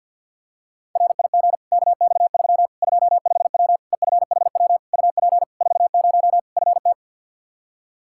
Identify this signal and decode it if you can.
Morse code, heard as a keyed tone, about 50 wpm, 700 Hz. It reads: RIG KX3 25W EFHW UP 40 FT
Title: Practice QSO - Exchange 2 - Info - 50wpm